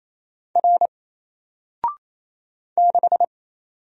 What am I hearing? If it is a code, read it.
Morse code, heard as a keyed tone, about 29 wpm, 700 Hz. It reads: R 6
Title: Single Character (Letter:Number:Punctuation:Prosign) 28wpm